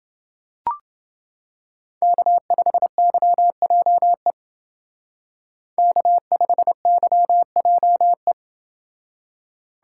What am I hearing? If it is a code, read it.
Morse code, heard as a keyed tone, about 28 wpm, 700 Hz. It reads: K5YJE K5YJE